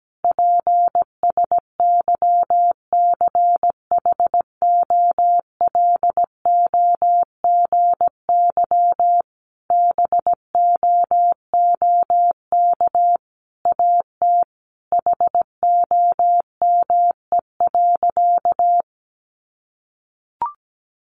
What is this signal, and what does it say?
Morse code, heard as a keyed tone, about 17 wpm, 700 Hz. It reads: PSYCHOLOGY BOOK AT HOME.
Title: Sentences from Top 1400 Words 17wpm